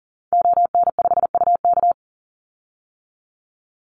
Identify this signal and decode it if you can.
Morse code, heard as a keyed tone, about 40 wpm, 700 Hz. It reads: ON5UK
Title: Call Signs - 40wpm